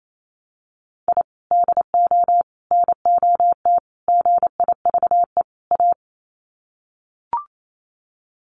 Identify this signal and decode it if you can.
Morse code, heard as a keyed tone, about 28 wpm, 700 Hz. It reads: I DO NOT GIVE A
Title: Sets of 5 Words Encore2 28wpm